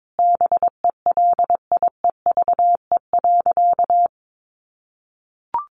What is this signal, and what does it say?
Morse code, heard as a keyed tone, about 22 wpm, 700 Hz. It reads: BELIEVE.
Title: Sentences from Top 1000 Words 22wpm